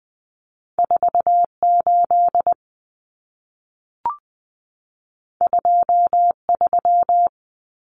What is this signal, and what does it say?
Morse code, heard as a keyed tone, about 20 wpm, 700 Hz. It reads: 48 23
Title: Common Cut Numbers - 2-Digit 20wpm